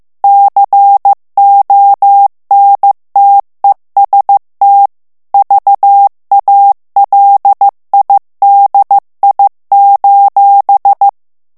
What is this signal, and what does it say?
Morse code, heard as a keyed tone, about 15 wpm, 800 Hz. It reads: CONTEST VALIDI:
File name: morse mp3 / ITALIEN texte clair 2 vitesse 14 mots minute